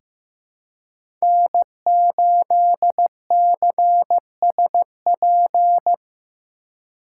Morse code, heard as a keyed tone, about 15 wpm, 700 Hz. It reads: N8CSP
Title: USA Call Signs - 5 Characters - 4x Character Spacing - 15wpm